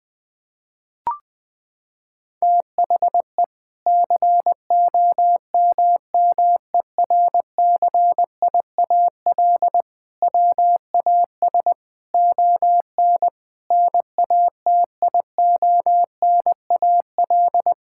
Morse code, heard as a keyed tone, about 21 wpm, 700 Hz. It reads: THE COMMERCIAL WAS ON NATIONAL
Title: Sentences from Top 1000 Words - Review Part 5 - 20wpm